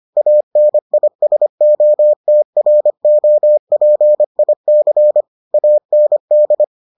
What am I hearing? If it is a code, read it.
Morse code, heard as a keyed tone, about 25 wpm, 600 Hz. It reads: ANISOTROPIC AND